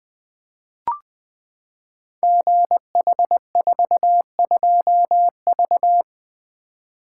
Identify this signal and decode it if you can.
Morse code, heard as a keyed tone, about 20 wpm, 700 Hz. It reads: GH42V
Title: Code Groups - 5 Letter:Numbers - 20wpm